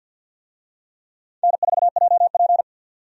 Morse code, heard as a keyed tone, about 48 wpm, 700 Hz. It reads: N4JP
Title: USA Call Signs - 4 Characters - 50wpm